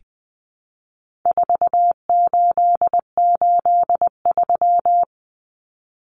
Morse code, heard as a keyed tone, about 19 wpm, 700 Hz. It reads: 4883
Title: Numbers - 4-Digit 20wpm